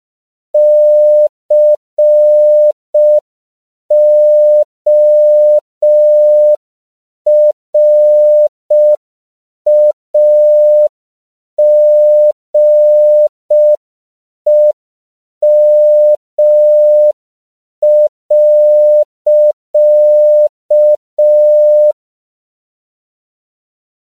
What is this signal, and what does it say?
Morse code, heard as a keyed tone, about 5 wpm, 600 Hz. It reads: CORAGEM.
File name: 5PPM